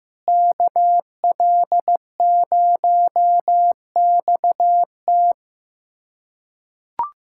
Morse code, heard as a keyed tone, about 15 wpm, 700 Hz. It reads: KL0XT
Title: USA Callsign DE Callsign - 5 Characters - 15wpm